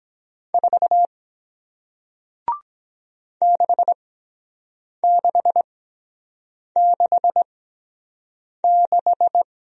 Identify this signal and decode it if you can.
Morse code, heard as a keyed tone, about 22 wpm, spaced as for 7 wpm, 700 Hz. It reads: 4 6666